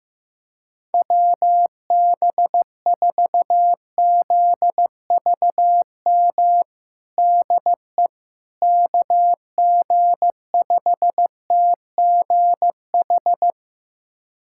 Morse code, heard as a keyed tone, about 15 wpm, 700 Hz. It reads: WB4ZVM DE KG5TGH
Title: USA Callsign DE Callsign - 6 Characters - 15wpm